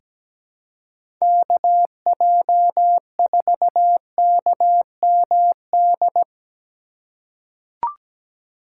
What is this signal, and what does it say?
Morse code, heard as a keyed tone, about 17 wpm, 700 Hz. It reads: KJ4KMD
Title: USA Call Signs - 6 Characters - 2x Character Spacing - 17wpm